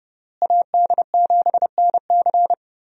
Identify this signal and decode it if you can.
Morse code, heard as a keyed tone, about 32 wpm, 700 Hz. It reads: AD7NC
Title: USA Call Signs - 5 Characters - Repeated 3x - 30wpm